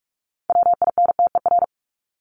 Morse code, heard as a keyed tone, about 44 wpm, 700 Hz. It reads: WINTER
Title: Top 1501-1750 Words 45wpm